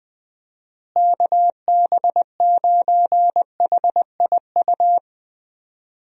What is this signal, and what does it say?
Morse code, heard as a keyed tone, about 20 wpm, 700 Hz. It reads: KB9HIU
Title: USA Call Signs - 6 Characters - Repeated 3x - 20wpm